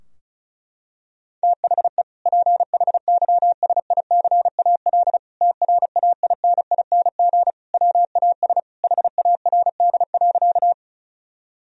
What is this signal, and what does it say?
Morse code, heard as a keyed tone, about 36 wpm, 700 Hz. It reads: THE PHYSICAL TRAINING WAS HARD.
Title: Sentences from Top 1000 Words - Review Part 4 - 35wpm